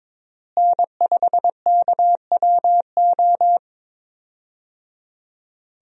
Morse code, heard as a keyed tone, about 22 wpm, 700 Hz. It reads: N5KWO
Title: USA Call Signs - 5 Characters - Repeated 3x - 22wpm